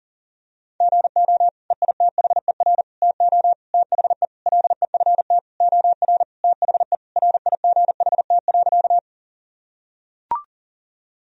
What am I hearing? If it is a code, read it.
Morse code, heard as a keyed tone, about 40 wpm, 700 Hz. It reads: GO EITHER TO THE LEFT OR THE RIGHT.
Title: Sentences from Top 1000 Words - Review Part 4 - 40wpm